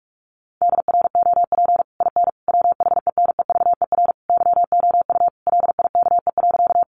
Morse code, heard as a keyed tone, about 44 wpm, 700 Hz. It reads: DROP IN WHENEVER YOU LIKE.
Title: Sentences from Top 1400 Words 45wpm